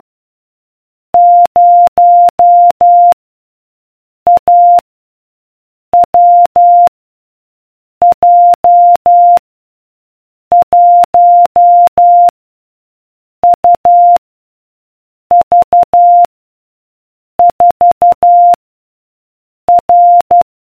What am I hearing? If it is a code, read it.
Morse code, heard as a keyed tone, about 12 wpm, 700 Hz. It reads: 0 A W J 1 U V 4 R